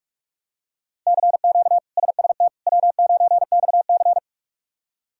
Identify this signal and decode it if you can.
Morse code, heard as a keyed tone, about 46 wpm, 700 Hz. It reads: CQ SST W9XC
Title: K1USN SST QSO Practice - 45wpm